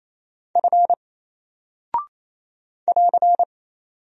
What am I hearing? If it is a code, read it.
Morse code, heard as a keyed tone, about 27 wpm, 700 Hz. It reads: F <AR>
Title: Single Character (Letter:Number:Punctuation:Prosign) 28wpm